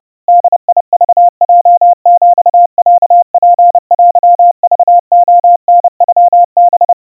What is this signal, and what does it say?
Morse code (Japanese Wabun code), heard as a keyed tone, about 30 wpm, 700 Hz. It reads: ボクヲヒロツテクレタノハ
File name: 110 cpm 700 Hz Crispy